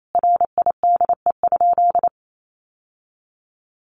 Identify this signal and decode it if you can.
Morse code, heard as a keyed tone, about 28 wpm, 700 Hz. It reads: RIDE?
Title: Sentences from Top 1200 Words 28wpm